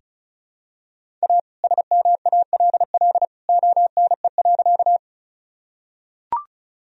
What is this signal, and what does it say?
Morse code, heard as a keyed tone, about 36 wpm, 700 Hz. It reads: A SMALL ONE.
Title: Sentences from Top 1000 Words - Review Part 1 - 35wpm